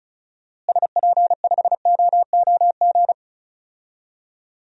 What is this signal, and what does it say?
Morse code, heard as a keyed tone, about 36 wpm, 700 Hz. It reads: SP5OOG
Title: DXCC-100 Call Signs - 6 Characters - Repeated 3x - 35wpm